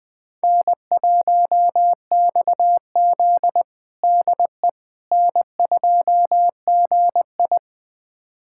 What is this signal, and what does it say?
Morse code, heard as a keyed tone, about 21 wpm, 700 Hz. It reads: N1XZ DE N2GI
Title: USA Callsign DE Callsign - 4 Characters - 20wpm